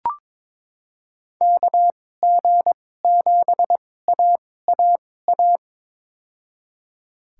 Morse code, heard as a keyed tone, about 22 wpm, spaced as for 16 wpm, 700 Hz. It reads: KG7AAA
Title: USA Call Signs - 2x Character Spacing - 22wpm